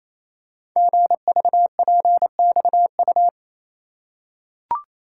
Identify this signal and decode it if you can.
Morse code, heard as a keyed tone, about 27 wpm, 700 Hz. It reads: GVPXU
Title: Code Groups - 5 Letters - 28wpm